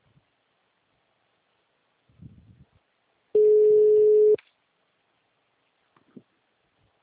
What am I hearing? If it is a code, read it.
Ringback tone (Europe, 425 Hz).
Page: Mogi Guacu - SP